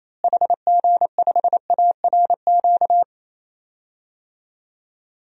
Morse code, heard as a keyed tone, about 28 wpm, 700 Hz. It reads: HG5ARQ
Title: DXCC-100 Call Signs - 6 Characters - Repeated 2x - 28wpm